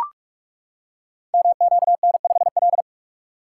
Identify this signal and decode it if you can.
Morse code, heard as a keyed tone, about 44 wpm, 700 Hz. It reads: MQN5L
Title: Code Groups - 5 Letter:Numbers - 45wpm